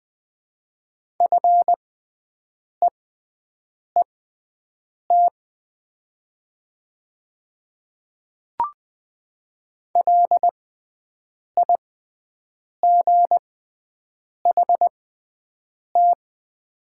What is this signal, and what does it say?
Morse code, heard as a keyed tone, about 20 wpm, spaced as for 7 wpm, 700 Hz. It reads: FEET LIGHT
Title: 6x Character Spacing - Top 300 Words - 20wpm